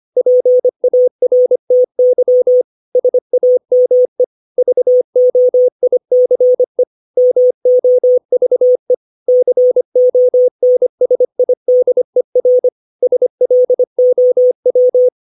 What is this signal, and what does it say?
Morse code, heard as a keyed tone, about 25 wpm, 500 Hz. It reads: PARTY SAME VOICE MOVE CONSIDER SLOW
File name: 632 words / 632 Words 25WPM